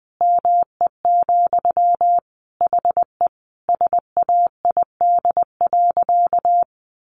Morse code, heard as a keyed tone, about 20 wpm, 700 Hz. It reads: ME, HE SAID.
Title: Sentences from Top 600 Words 20wpm